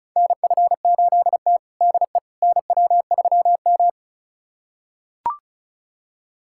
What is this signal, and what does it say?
Morse code, heard as a keyed tone, about 34 wpm, 700 Hz. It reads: NF8T DE NW3M
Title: USA Callsign DE Callsign - 4 Characters - 35wpm